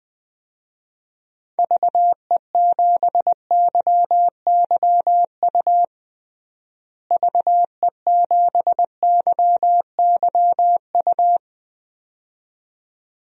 Morse code, heard as a keyed tone, about 20 wpm, 700 Hz. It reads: VE7YYU VE7YYU
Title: DXCC-100 Call Signs - 6 Characters - Repeated 2x - 20wpm